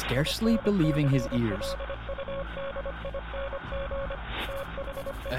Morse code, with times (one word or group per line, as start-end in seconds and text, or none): 0.2-4.2 INCLUDING
4.5-5.3 THE